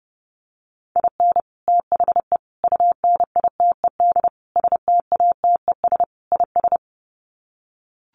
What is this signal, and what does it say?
Morse code, heard as a keyed tone, about 30 wpm, 700 Hz. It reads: IN THE UNITED STATES IS
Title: Sets of 5 Words 30wpm